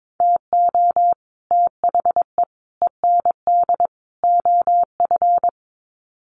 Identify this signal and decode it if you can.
Morse code, heard as a keyed tone, about 22 wpm, 700 Hz. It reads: TO THE END OF
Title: Sets of 4 Words 22wpm